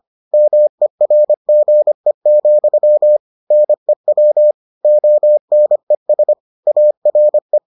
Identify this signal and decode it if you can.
Morse code, heard as a keyed tone, about 25 wpm, 600 Hz. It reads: MERGE, NEW ONES ARE